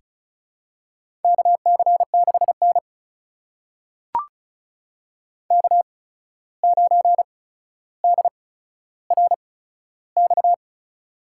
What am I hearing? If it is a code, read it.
Morse code, heard as a keyed tone, about 34 wpm, spaced as for 14 wpm, 700 Hz. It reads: KC6N K9DRX
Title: USA Call Signs - 8x Character Spacing - 35wpm